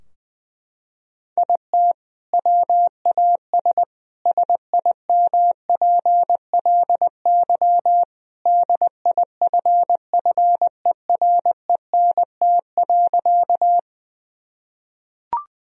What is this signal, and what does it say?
Morse code, heard as a keyed tone, about 20 wpm, 700 Hz. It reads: IT WAS SIMPLY DIFFERENT.
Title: Sentences from Top 1000 Words - Review Part 4 - 20wpm